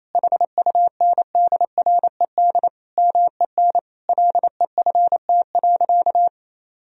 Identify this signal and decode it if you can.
Morse code, heard as a keyed tone, about 28 wpm, 700 Hz. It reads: HUNDRED MEN LEFT.
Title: Sentences from Top 1000 Words - Review Part 2 - 28wpm